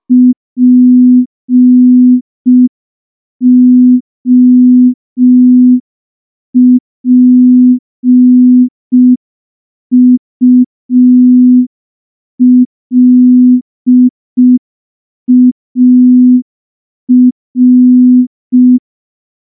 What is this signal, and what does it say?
Morse code, heard as a keyed tone, about 5 wpm, 250 Hz. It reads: POPULAR
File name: morse / morse-training-05WPM